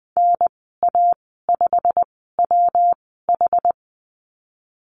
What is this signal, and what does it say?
Morse code, heard as a keyed tone, about 20 wpm, spaced as for 14 wpm, 700 Hz. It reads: NA5WH
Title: 122 Lesson 20g - Number 3 - Call Signs 20wpm